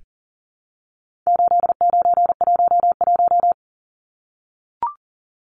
Morse code, heard as a keyed tone, about 40 wpm, 700 Hz. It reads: 8911
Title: Numbers - 4-Digit 40wpm